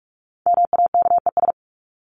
Morse code, heard as a keyed tone, about 44 wpm, 700 Hz. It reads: MAKES
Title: ES:S Suffix - Top 100 Words with Base Words 45wpm